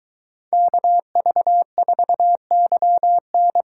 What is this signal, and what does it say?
Morse code, heard as a keyed tone, about 23 wpm, 700 Hz. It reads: KV4YN